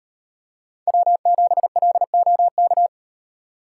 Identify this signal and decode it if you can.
Morse code, heard as a keyed tone, about 38 wpm, 700 Hz. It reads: W7LOK